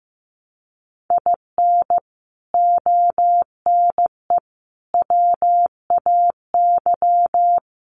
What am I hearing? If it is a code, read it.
Morse code, heard as a keyed tone, about 15 wpm, 700 Hz. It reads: IN ONE WAY
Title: Sets of 5 Words Encore 15wpm